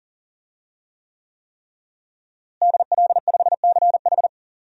Morse code, heard as a keyed tone, about 40 wpm, 700 Hz. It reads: DL5CH
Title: DXCC-100 Call Signs - 5 Characters - Repeated 3x - 40wpm